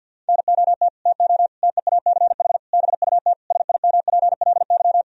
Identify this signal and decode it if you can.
Morse code, heard as a keyed tone, about 50 wpm, 700 Hz. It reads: NOT TO TEACH BUT SIMPLY
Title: Sentences from Top 1000 Words - Review Part 4 - 50wpm